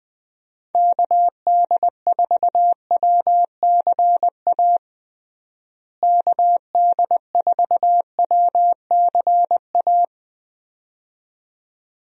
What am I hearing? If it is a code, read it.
Morse code, heard as a keyed tone, about 20 wpm, 700 Hz. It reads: KD4WCA KD4WCA
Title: USA Call Signs - 6 Characters - Repeated 2x - 20wpm